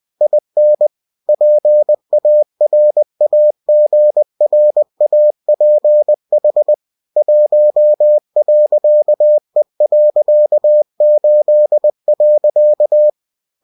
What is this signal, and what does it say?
Morse code, heard as a keyed tone, about 20 wpm, 600 Hz. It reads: IN PARAGRAPH 1.E.8.
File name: cw-books / winnie-the-pooh / winnie-20wpm-page-0113